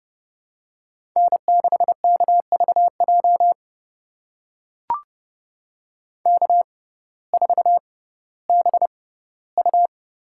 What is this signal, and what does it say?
Morse code, heard as a keyed tone, about 30 wpm, spaced as for 17 wpm, 700 Hz. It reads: N6KVJ K4BU
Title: USA Call Signs - 5 Characters - 6x Character Spacing - 30wpm